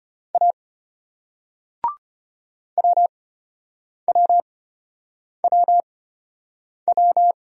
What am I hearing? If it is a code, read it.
Morse code, heard as a keyed tone, about 34 wpm, spaced as for 8 wpm, 700 Hz. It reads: A WWWW